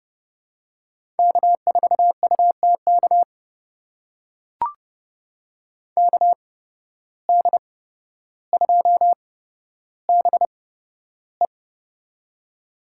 Morse code, heard as a keyed tone, about 30 wpm, spaced as for 13 wpm, 700 Hz. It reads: K4UTK KD2BE
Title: USA Call Signs - 5 Characters - 8x Character Spacing - 30wpm